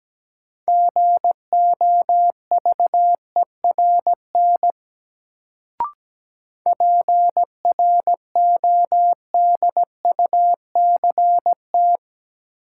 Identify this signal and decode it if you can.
Morse code, heard as a keyed tone, about 17 wpm, 700 Hz. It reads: GOVERN PRODUCT
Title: Sentences from Top 1000 Words - Review Part 2 - New Word Drill - 17wpm